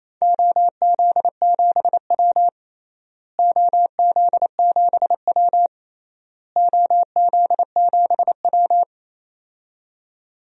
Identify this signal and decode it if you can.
Morse code, heard as a keyed tone, about 28 wpm, 700 Hz. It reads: OZ7W OZ7W OZ7W
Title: DXCC-100 Call Signs - 4 Characters - Repeated 3x - 28wpm